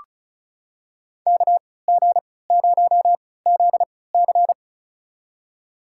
Morse code, heard as a keyed tone, about 36 wpm, spaced as for 20 wpm, 700 Hz. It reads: KG0ZC
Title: USA Call Signs - 5 Characters - 3x Character Spacing - 35wpm